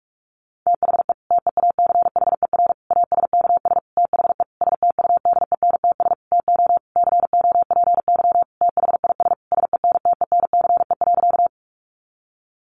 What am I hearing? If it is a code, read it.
Morse code, heard as a keyed tone, about 44 wpm, 700 Hz. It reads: THE TEACHER ASKS THE STUDENTS TO COPY THIS SENTENCE.
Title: Additional Easy Phrases 45wpm